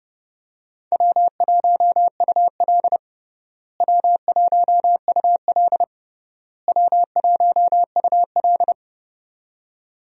Morse code, heard as a keyed tone, about 30 wpm, 700 Hz. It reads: W1UL W1UL W1UL
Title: USA Call Signs - 4 Characters - Repeated 3x - 30wpm